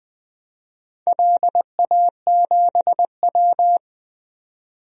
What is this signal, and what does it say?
Morse code, heard as a keyed tone, about 20 wpm, 700 Hz. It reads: LA7W
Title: DXCC-100 Call Signs - 4 Characters - 20wpm